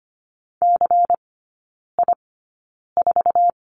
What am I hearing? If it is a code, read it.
Morse code, heard as a keyed tone, about 25 wpm, spaced as for 9 wpm, 700 Hz. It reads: CI4
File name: course / 25:10wpm / 093 Lesson 16e - Letter c - Three Characters 25wpm